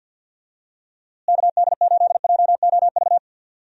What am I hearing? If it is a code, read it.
Morse code, heard as a keyed tone, about 49 wpm, 700 Hz. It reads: KD8JOV
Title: USA Call Signs - 6 Characters - Repeated 3x - 50wpm